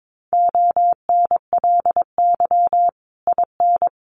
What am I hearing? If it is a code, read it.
Morse code, heard as a keyed tone, about 22 wpm, 700 Hz. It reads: ONLY IN
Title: Sentences from Top 1300 Words 22wpm